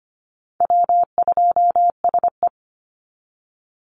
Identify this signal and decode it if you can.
Morse code, heard as a keyed tone, about 25 wpm, 700 Hz. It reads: W2SE
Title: NA SOTA Chaser Callsigns 25wpm